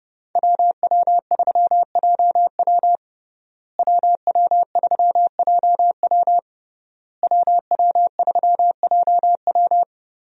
Morse code, heard as a keyed tone, about 30 wpm, 700 Hz. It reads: WW3JW WW3JW WW3JW
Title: USA Call Signs - 5 Characters - Repeated 3x - 30wpm